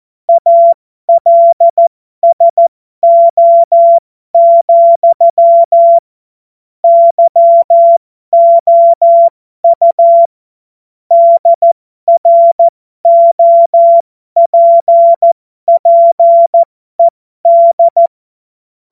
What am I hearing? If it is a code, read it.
Morse code, heard as a keyed tone, about 14 wpm, spaced as for 12 wpm, 680 Hz. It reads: ALSO, YOU DROPPED
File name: Morsekurs / Witze